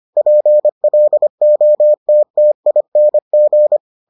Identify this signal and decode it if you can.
Morse code, heard as a keyed tone, about 25 wpm, 600 Hz. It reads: PLOTTING